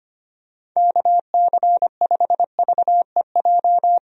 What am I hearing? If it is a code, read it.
Morse code, heard as a keyed tone, about 25 wpm, 700 Hz. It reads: KC5VEJ
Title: USA Call Signs - 6 Characters - 25wpm